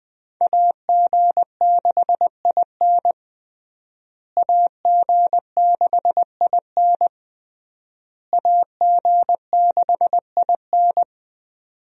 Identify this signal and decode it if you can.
Morse code, heard as a keyed tone, about 20 wpm, 700 Hz. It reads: AG6IN AG6IN AG6IN
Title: USA Call Signs - 5 Characters - Repeated 3x - 20wpm